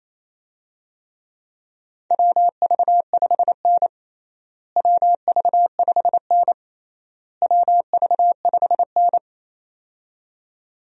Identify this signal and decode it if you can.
Morse code, heard as a keyed tone, about 28 wpm, 700 Hz. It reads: WV5N WV5N WV5N
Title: USA Call Signs - 4 Characters - Repeated 3x - 28wpm